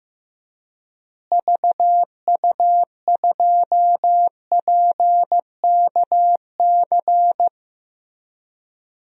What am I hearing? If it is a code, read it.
Morse code, heard as a keyed tone, about 16 wpm, 700 Hz. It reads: VU2PKC
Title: DXCC-100 Call Signs - 6 Characters - Repeated 3x - 15wpm